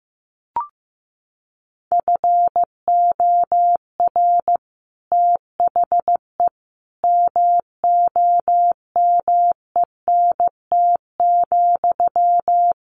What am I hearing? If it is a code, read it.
Morse code, heard as a keyed tone, about 15 wpm, 700 Hz. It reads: FOR THE MOMENT,
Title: Sentences from Top 1400 Words 15wpm